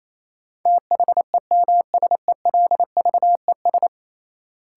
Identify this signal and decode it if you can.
Morse code, heard as a keyed tone, about 28 wpm, 700 Hz. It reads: THEMSELVES
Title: Sentences from Top 800 Words - New Word Drill - 28wpm